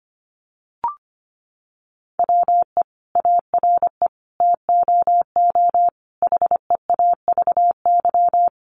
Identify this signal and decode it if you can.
Morse code, heard as a keyed tone, about 25 wpm, 700 Hz. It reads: WE ARE TOO HEAVY
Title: Sentences from Top 1100 Words 25wpm